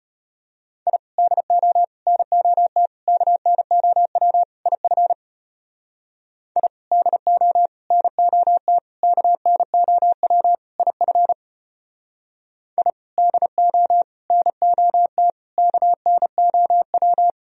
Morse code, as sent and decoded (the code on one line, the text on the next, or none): .. / -.. --- / -. --- - / -.- -. --- .-- / .. ..-. / .. / -.. --- / -. --- - / -.- -. --- .-- / .. ..-. / .. / -.. --- / -. --- - / -.- -. --- .--
I DO NOT KNOW IF I DO NOT KNOW IF I DO NOT KNOW